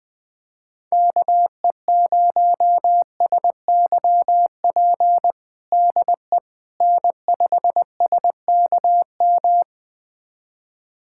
Morse code, heard as a keyed tone, about 20 wpm, 700 Hz. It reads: KE0SYP DE N5SKM
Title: USA Callsign DE Callsign 20wpm